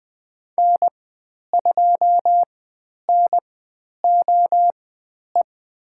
Morse code, heard as a keyed tone, about 20 wpm, spaced as for 10 wpm, 700 Hz. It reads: N2NOE
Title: 109 Lesson 18g - Letter m - Call Signs 20wpm